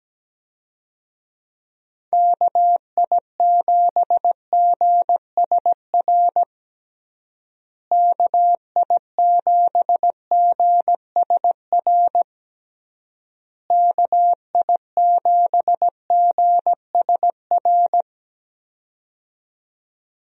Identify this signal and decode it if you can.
Morse code, heard as a keyed tone, about 17 wpm, 700 Hz. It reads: KI7GSR KI7GSR KI7GSR
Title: USA Call Signs - 6 Characters - Repeated 3x - 17wpm